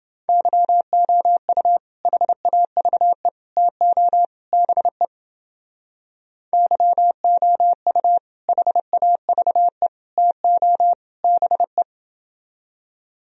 Morse code, heard as a keyed tone, about 30 wpm, 700 Hz. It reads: YOU HAVE TO BE YOU HAVE TO BE